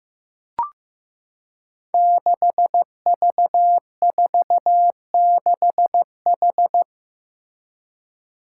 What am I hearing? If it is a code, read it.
Morse code, heard as a keyed tone, about 15 wpm, 700 Hz. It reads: 6V46H
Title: Five Challenging Letter-Number Combinations 15wpm